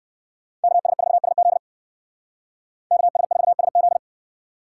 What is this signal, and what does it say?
Morse code, heard as a keyed tone, about 57 wpm, 700 Hz. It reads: KS4SZ KS4SZ